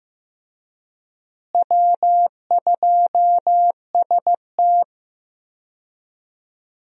Morse code, heard as a keyed tone, about 15 wpm, 700 Hz. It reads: W2ST
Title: USA Call Signs - 4 Characters - Repeated 2x - 15wpm